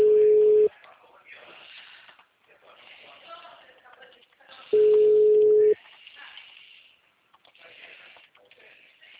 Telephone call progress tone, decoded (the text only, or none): ringback tone